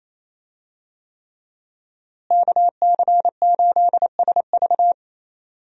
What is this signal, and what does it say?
Morse code, heard as a keyed tone, about 27 wpm, 700 Hz. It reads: KC8SV
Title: USA Call Signs - 5 Characters - Repeated 3x - 28wpm